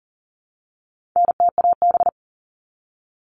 Morse code, heard as a keyed tone, about 40 wpm, 700 Hz. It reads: NTAB
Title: All Cut Numbers - 4-Digit 40wpm